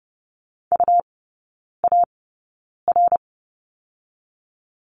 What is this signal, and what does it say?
Morse code, heard as a keyed tone, about 30 wpm, spaced as for 9 wpm, 700 Hz. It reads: UAR
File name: Lesson 17e - Letter u - Three Characters 30wpm